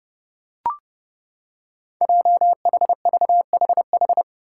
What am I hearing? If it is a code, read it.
Morse code, heard as a keyed tone, about 32 wpm, 700 Hz. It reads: JHVHH
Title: Five Challenging Letter-Number Combinations 30wpm